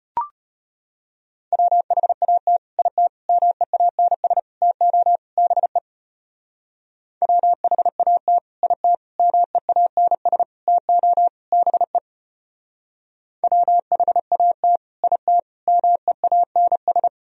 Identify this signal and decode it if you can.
Morse code, heard as a keyed tone, about 29 wpm, 700 Hz. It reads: WHAT IT MEANS TO BE WHAT IT MEANS TO BE WHAT IT MEANS
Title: Speed-Racing - Sets of 5 Words 25wpm